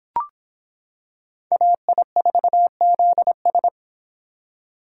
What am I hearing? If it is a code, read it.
Morse code, heard as a keyed tone, about 26 wpm, 700 Hz. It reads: AI4ZS